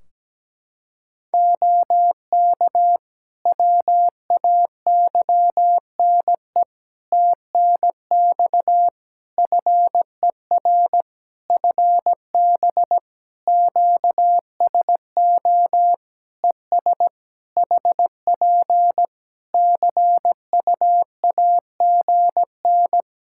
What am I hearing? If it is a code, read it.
Morse code, heard as a keyed tone, about 17 wpm, 700 Hz. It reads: OK WAYNE TNX FER FB QSO ES HP CUAGN
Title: Practice QSO - Exchange 4 - Ending - 17wpm